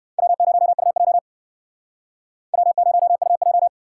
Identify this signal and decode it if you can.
Morse code, heard as a keyed tone, about 57 wpm, 700 Hz. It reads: W0UJ W0UJ